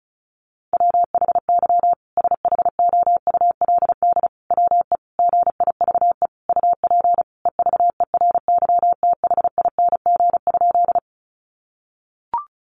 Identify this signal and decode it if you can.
Morse code, heard as a keyed tone, about 34 wpm, 700 Hz. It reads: WHY SHOULD WE GIVE UP EVERYTHING?
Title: Sentences from Top 700 Words 35wpm